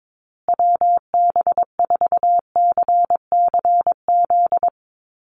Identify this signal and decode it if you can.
Morse code, heard as a keyed tone, about 22 wpm, 700 Hz. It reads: WB4CCZ
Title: USA Call Signs Repeated 2x 22wpm